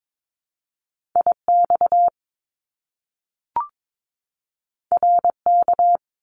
Morse code, heard as a keyed tone, about 22 wpm, 700 Hz. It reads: IX RK
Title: Top Two Letter Combinations in Words 22wpm